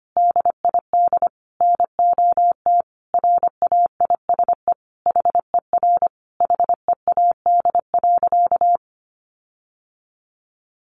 Morse code, heard as a keyed tone, about 25 wpm, 700 Hz. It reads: DID NOT RAISE HER HEAD.
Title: Sentences from Top 1400 Words 25wpm